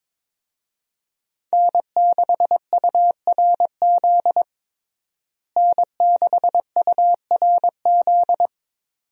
Morse code, heard as a keyed tone, about 22 wpm, 700 Hz. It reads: N6URZ N6URZ
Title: USA Call Signs - 5 Characters - Repeated 2x - 22wpm